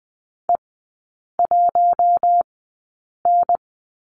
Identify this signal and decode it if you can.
Morse code, heard as a keyed tone, about 20 wpm, spaced as for 8 wpm, 690 Hz. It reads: E1N
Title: 087 Lesson 15e - Number 5 - Three Characters 20wpm